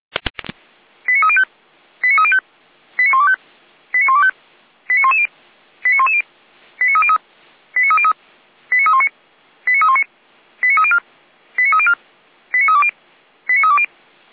Five-tone selective call (ZVEI), heard as quick runs of five tones, tones of about 70 ms. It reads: 89305 89305 89125 89125 89110 89110 89303 89303 89319 89319 89304 89304 89320 89320
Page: /455/SDR/ZVEI/